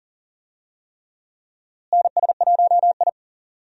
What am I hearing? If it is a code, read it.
Morse code, heard as a keyed tone, about 40 wpm, 700 Hz. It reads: NS1I
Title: USA Call Signs - 4 Characters - Repeated 3x - 40wpm